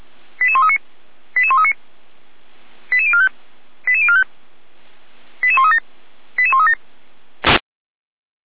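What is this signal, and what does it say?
Five-tone selective call (ZVEI), heard as quick runs of five tones, tones of about 70 ms. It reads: 88139 88139 88045 88045 88137 88137
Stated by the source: ZVEI five-tone call received by SDR.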